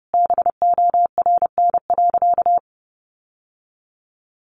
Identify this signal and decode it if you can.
Morse code, heard as a keyed tone, about 30 wpm, 700 Hz. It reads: BORN.
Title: Sentences from Top 1200 Words 30wpm